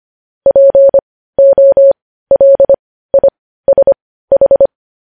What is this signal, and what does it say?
Morse code, heard as a keyed tone, about 25 wpm, spaced as for 15 wpm, 550 Hz. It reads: POLISH
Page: News Headlines in Morse Code at 15 WPM